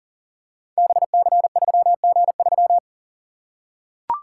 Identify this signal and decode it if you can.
Morse code, heard as a keyed tone, about 40 wpm, 700 Hz. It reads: BC3G3
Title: Code Groups - 5 Letter:Numbers - 40wpm